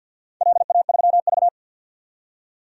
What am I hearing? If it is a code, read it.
Morse code, heard as a keyed tone, about 51 wpm, 700 Hz. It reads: LA3V
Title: DXCC-100 Call Signs - 4 Characters - 50wpm